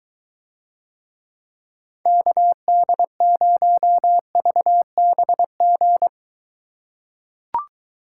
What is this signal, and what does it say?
Morse code, heard as a keyed tone, about 23 wpm, 700 Hz. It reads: KD0VBG